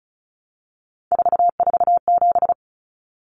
Morse code, heard as a keyed tone, about 34 wpm, 700 Hz. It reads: 447
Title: Common Cut Numbers - 3-Digit 35wpm